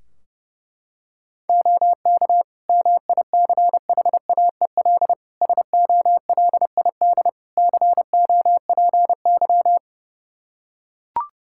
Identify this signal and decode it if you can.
Morse code, heard as a keyed tone, about 32 wpm, 700 Hz. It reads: OK MICHAEL SOLID COPY
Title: K1USN SST QSO Practice 30wpm